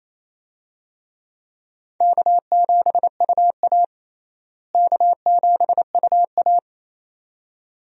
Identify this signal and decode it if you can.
Morse code, heard as a keyed tone, about 28 wpm, 700 Hz. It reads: K7UA K7UA
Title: USA Call Signs - 4 Characters - Repeated 2x - 28wpm